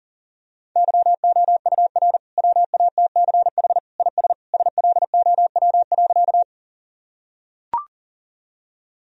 Morse code, heard as a keyed tone, about 40 wpm, 700 Hz. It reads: YOUR WATCH IS SLOW.
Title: Sentences from Top 1000 Words - Review Part 2 - 40wpm